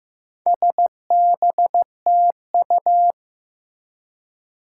Morse code, heard as a keyed tone, about 15 wpm, 700 Hz. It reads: SBTU
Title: Four Challenging Letter-Number Combinations 15wpm